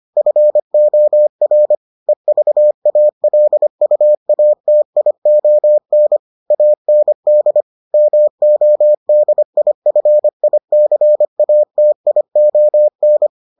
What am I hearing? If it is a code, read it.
Morse code, heard as a keyed tone, about 25 wpm, 600 Hz. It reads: FOR EVALUATION AND MODIFICATION